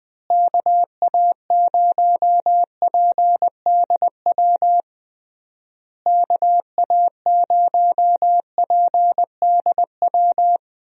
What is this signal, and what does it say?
Morse code, heard as a keyed tone, about 21 wpm, 700 Hz. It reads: KA0PDW KA0PDW
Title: USA Call Signs - 6 Characters - Repeated 3x - 20wpm